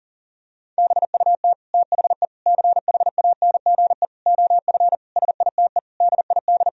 Morse code, heard as a keyed tone, about 40 wpm, 700 Hz. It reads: BUT THE CHANGE OF SITE DID
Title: Sentences from Top 1000 Words - Review Part 4 - 40wpm